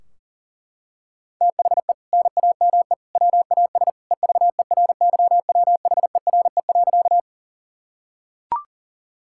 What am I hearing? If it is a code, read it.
Morse code, heard as a keyed tone, about 40 wpm, 700 Hz. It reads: THE NAME WAS EVERYWHERE.
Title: Sentences from Top 1600 Words 40wpm